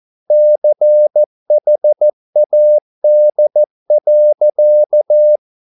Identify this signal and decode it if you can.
Morse code, heard as a keyed tone, about 14 wpm, 600 Hz. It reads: CHAD.